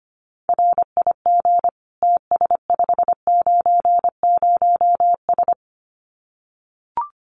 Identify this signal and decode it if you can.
Morse code, heard as a keyed tone, about 25 wpm, 700 Hz. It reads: RIG TS590S
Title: Common QSO Phrases Encore 25wpm